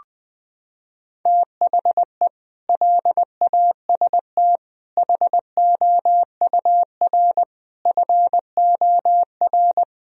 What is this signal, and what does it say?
Morse code, heard as a keyed tone, about 19 wpm, 700 Hz. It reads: THE LAST HOUR FOR
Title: Sentences from Top 1000 Words - Review Part 4 - 20wpm